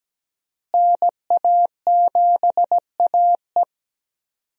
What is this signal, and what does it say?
Morse code, heard as a keyed tone, about 17 wpm, 700 Hz. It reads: NA7AE
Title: USA Call Signs - 5 Characters - Repeated 2x - 17wpm